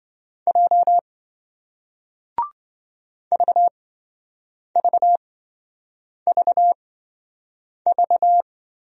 Morse code, heard as a keyed tone, about 28 wpm, spaced as for 7 wpm, 700 Hz. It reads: J VVVV